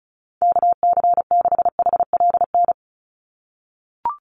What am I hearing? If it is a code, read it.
Morse code, heard as a keyed tone, about 34 wpm, 700 Hz. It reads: KC6HLN
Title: USA Call Signs 35wpm